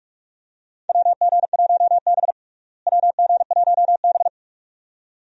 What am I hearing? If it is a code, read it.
Morse code, heard as a keyed tone, about 44 wpm, 700 Hz. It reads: WG1B WG1B
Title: USA Call Signs - 4 Characters - Repeated 2x - 45wpm